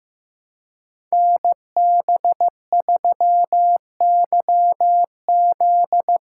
Morse code, heard as a keyed tone, about 15 wpm, 700 Hz. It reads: NB3YZ
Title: Code Groups - 5 Letter:Numbers - 15wpm